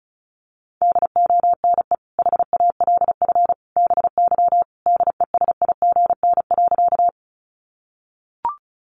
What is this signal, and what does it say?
Morse code, heard as a keyed tone, about 34 wpm, 700 Hz. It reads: DONE HALF BY DESIGN.
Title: Sentences from Top 800 Words 35wpm